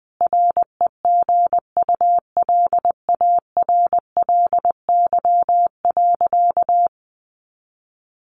Morse code, heard as a keyed tone, about 20 wpm, 700 Hz. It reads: REGULARLY.
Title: Sentences from Top 1400 Words 20wpm